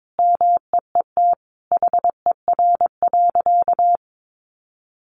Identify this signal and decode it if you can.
Morse code, heard as a keyed tone, about 22 wpm, 700 Hz. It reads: MEET HER.
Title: Sentences from Top 1100 Words 22wpm